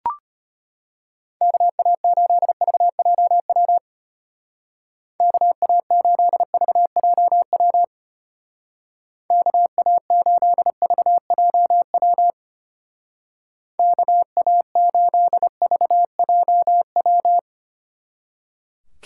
Morse code, heard as a keyed tone, about 30 wpm, 700 Hz. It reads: KA8VJW KA8VJW KA8VJW KA8VJW
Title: Speed Racing - USA Call Signs 25wpm